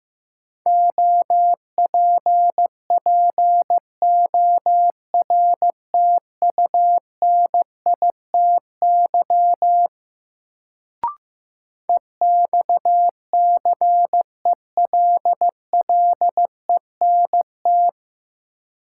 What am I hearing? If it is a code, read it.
Morse code, heard as a keyed tone, about 15 wpm, 700 Hz. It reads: OPPORTUNITY EXCELLENT
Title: Sentences from Top 1000 Words - New Word Drill - 15wpm